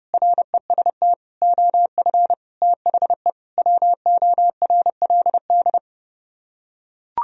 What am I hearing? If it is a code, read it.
Morse code, heard as a keyed tone, about 32 wpm, 700 Hz. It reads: REST OF THE WORLD